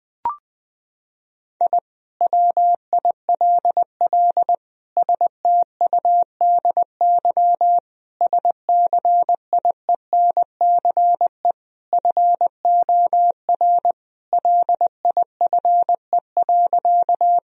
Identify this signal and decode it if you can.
Morse code, heard as a keyed tone, about 20 wpm, 700 Hz. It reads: I WILL STUDY SCIENCE FOR LIFE.
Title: Sentences from Top 1000 Words - Review Part 2 - 20wpm